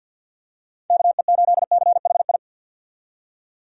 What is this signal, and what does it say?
Morse code, heard as a keyed tone, about 51 wpm, 700 Hz. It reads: KE8CHI
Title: USA Call Signs - 6 Characters - 50wpm